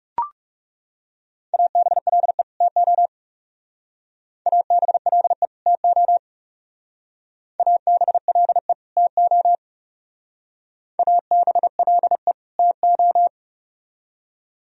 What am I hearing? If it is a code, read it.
Morse code, heard as a keyed tone, about 38 wpm, 700 Hz. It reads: ABLE TO ABLE TO ABLE TO ABLE TO